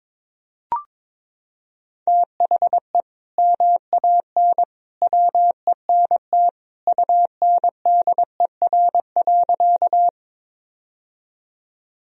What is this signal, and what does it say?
Morse code, heard as a keyed tone, about 22 wpm, 700 Hz. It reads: THE MAN WENT UNDER.
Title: Sentences from Top 1000 Words - Review Part 1 - 22wpm